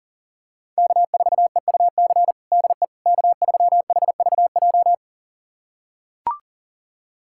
Morse code, heard as a keyed tone, about 40 wpm, 700 Hz. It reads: K4EUC DE K3HVJ
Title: USA Callsign DE Callsign - 5 Characters - 40wpm